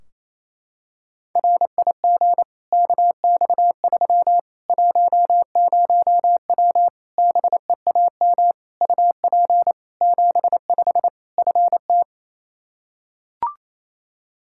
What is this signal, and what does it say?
Morse code, heard as a keyed tone, about 27 wpm, 700 Hz. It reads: RIG KX3 10W BEAM UP 75 FT
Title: Practice QSO - Exchange 2 - Info - 28wpm